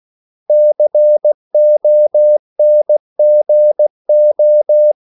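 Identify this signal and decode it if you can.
Morse code, heard as a keyed tone, about 16 wpm, 600 Hz. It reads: CONGO
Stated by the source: CW practice of national capitals at 16 words per minute, from ON6ZQ.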